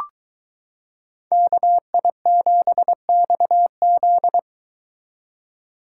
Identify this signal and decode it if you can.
Morse code, heard as a keyed tone, about 23 wpm, 700 Hz. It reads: KI7XZ